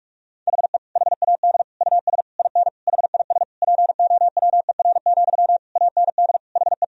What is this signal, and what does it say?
Morse code, heard as a keyed tone, about 42 wpm, 700 Hz. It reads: HE HAD US IN HIS POWER, AND HE
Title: Sentences from Top 1000 Words - Review Part 2 - 45wpm